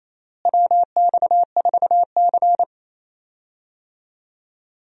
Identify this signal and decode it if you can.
Morse code, heard as a keyed tone, about 28 wpm, 700 Hz. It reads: WX4C
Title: USA Call Signs - 4 Characters - Repeated 2x - 28wpm